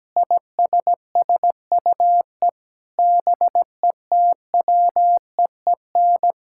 Morse code, heard as a keyed tone, about 17 wpm, 700 Hz. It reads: ISSUE BETWEEN
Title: Sentences from Top 1000 Words - Review Part 6 - 17wpm